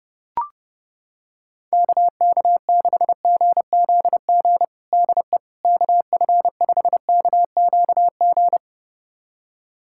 Morse code, heard as a keyed tone, about 29 wpm, 700 Hz. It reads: KK6GZG DE KF5KQG
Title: USA Callsign DE Callsign - 6 Characters - 30wpm